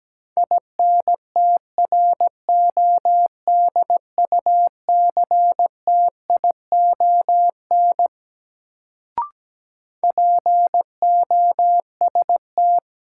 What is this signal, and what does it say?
Morse code, heard as a keyed tone, about 17 wpm, 700 Hz. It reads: INTRODUCTION POST
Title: Sentences from Top 1000 Words - New Word Drill - 17wpm